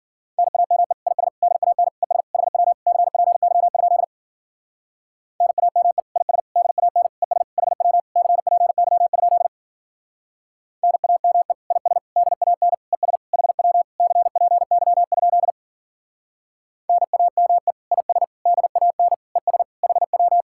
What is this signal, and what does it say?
Morse code, heard as a keyed tone, about 47 wpm, 700 Hz. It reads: NAME IS DAN ES HW CPY? NAME IS DAN ES HW CPY? NAME IS DAN ES HW CPY? NAME IS DAN ES HW
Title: Speed-Racing Common QSO Phrases - 40wpm